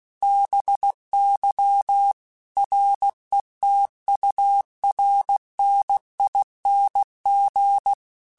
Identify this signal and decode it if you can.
Morse code, heard as a keyed tone, about 16 wpm, 800 Hz. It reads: BY RETURNING